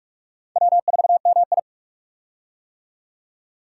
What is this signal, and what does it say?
Morse code, heard as a keyed tone, about 46 wpm, 700 Hz. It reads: W4MI
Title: USA Call Signs - 4 Characters - 45wpm